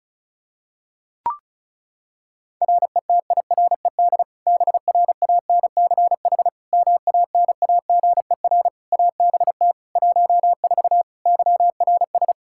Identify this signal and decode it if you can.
Morse code, heard as a keyed tone, about 34 wpm, 700 Hz. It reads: RETIRED BRANCH MANAGER ABT 14 YRS
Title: Practice QSO - Exchange 3 - More Info - 35wpm